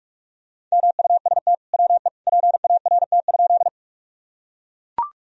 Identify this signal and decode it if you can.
Morse code, heard as a keyed tone, about 41 wpm, 700 Hz. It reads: MUST WE PART?
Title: Sentences from Top 1000 Words - Review Part 1 - 45wpm